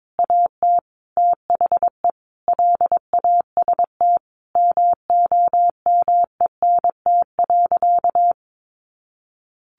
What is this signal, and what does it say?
Morse code, heard as a keyed tone, about 22 wpm, 700 Hz. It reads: AT THE LAST MOMENT.
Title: Sentences from Top 1400 Words 22wpm